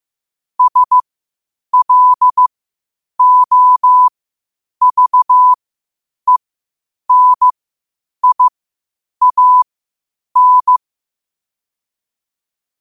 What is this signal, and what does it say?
Morse code, heard as a keyed tone, about 15 wpm, spaced as for 8 wpm, 1000 Hz. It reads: SLOVENIAN
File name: qotdfiles / 8wpm